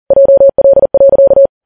Morse code, heard as a keyed tone, about 40 wpm, 550 Hz. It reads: JL.